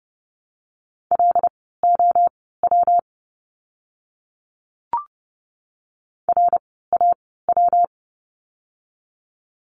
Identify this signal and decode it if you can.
Morse code, heard as a keyed tone, about 30 wpm, spaced as for 17 wpm, 700 Hz. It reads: LOW RAW
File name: course / 30:10wpm / 115 Lesson 19f - Letter w - Words 30wpm